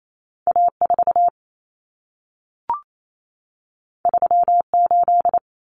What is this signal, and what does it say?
Morse code, heard as a keyed tone, about 28 wpm, 700 Hz. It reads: A4 38
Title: Common Cut Numbers - 2-Digit 28wpm